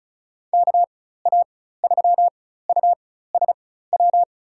Morse code, heard as a keyed tone, about 36 wpm, spaced as for 16 wpm, 700 Hz. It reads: KA3USW
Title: USA Call Signs - 6 Characters - 4x Character Spacing - 35wpm